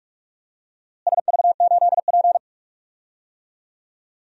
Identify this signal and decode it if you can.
Morse code, heard as a keyed tone, about 46 wpm, 700 Hz. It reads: SV8P
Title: DXCC-100 Call Signs - 4 Characters - Repeated 2x - 45wpm